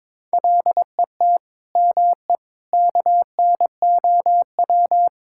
Morse code, heard as a keyed tone, about 22 wpm, 700 Hz. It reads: LET ME KNOW
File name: Sentences with 9-Word Phrases - 22wpm